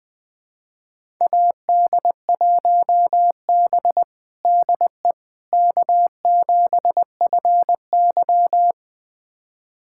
Morse code, heard as a keyed tone, about 20 wpm, 700 Hz. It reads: AD1B DE K7FY
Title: USA Callsign DE Callsign - 4 Characters - 20wpm